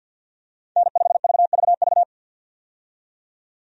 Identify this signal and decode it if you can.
Morse code, heard as a keyed tone, about 52 wpm, 700 Hz. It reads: N5VVV
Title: USA Call Signs - 5 Characters - 50wpm